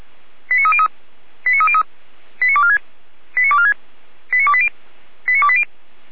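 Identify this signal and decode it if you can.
Five-tone selective call (ZVEI), heard as quick runs of five tones, tones of about 70 ms. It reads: 89303 89303 89246 89246 89280 89280
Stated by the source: ZVEI five-tone selective call for 89187, received by SDR.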